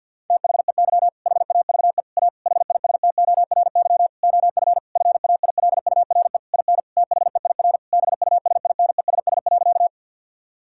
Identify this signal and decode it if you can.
Morse code, heard as a keyed tone, about 50 wpm, 700 Hz. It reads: THEY HAVE A HISTORY OF FAILURE IN THEIR BUSINESS.
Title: Sentences from Top 1000 Words - Review Part 6 - 50wpm